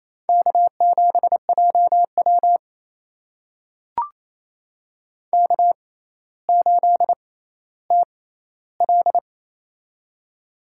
Morse code, heard as a keyed tone, about 28 wpm, spaced as for 14 wpm, 700 Hz. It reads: K7JW K8TL
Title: USA Call Signs - 4 Characters - 6x Character Spacing - 28wpm